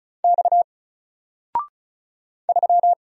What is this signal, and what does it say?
Morse code, heard as a keyed tone, about 36 wpm, 700 Hz. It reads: X 3
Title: Single Letter-Number 35wpm